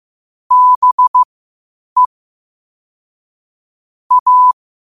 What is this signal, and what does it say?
Morse code, heard as a keyed tone, about 15 wpm, spaced as for 8 wpm, 1000 Hz. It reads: BE A
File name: qotdfiles / 8wpm